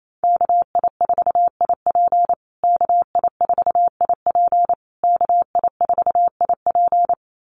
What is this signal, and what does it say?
Morse code, heard as a keyed tone, about 28 wpm, 700 Hz. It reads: KI4IP KI4IP KI4IP
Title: USA Call Signs Repeated 3x 28wpm